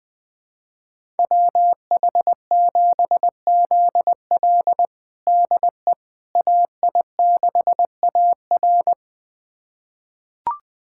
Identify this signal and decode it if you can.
Morse code, heard as a keyed tone, about 20 wpm, 700 Hz. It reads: WH7ZL DE AI6AR
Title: USA Callsign DE Callsign - 5 Characters - 20wpm